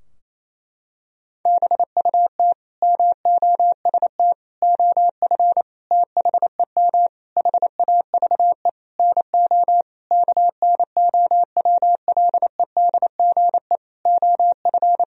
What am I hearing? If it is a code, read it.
Morse code, heard as a keyed tone, about 28 wpm, 700 Hz. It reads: BUT MOST OF THEM HAVE NO KNOWLEDGE OF
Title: Sentences from Top 1000 Words - Review Part 3 - 28wpm